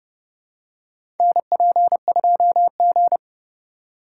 Morse code, heard as a keyed tone, about 30 wpm, 700 Hz. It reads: NP2G
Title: USA Call Signs - 4 Characters - 3x Character Spacing - 30wpm